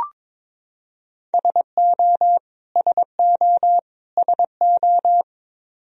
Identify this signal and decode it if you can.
Morse code, heard as a keyed tone, about 22 wpm, 700 Hz. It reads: SO SO SO
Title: Top 100 Words Repeated 22wpm